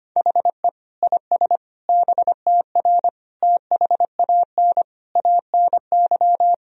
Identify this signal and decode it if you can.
Morse code, heard as a keyed tone, about 25 wpm, 700 Hz. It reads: HE IS BTR THAN ANY
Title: Sentences from Top 400 Words and Abbreviations 25wpm